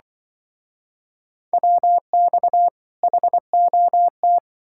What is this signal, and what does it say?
Morse code, heard as a keyed tone, about 27 wpm, 700 Hz. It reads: WX HOT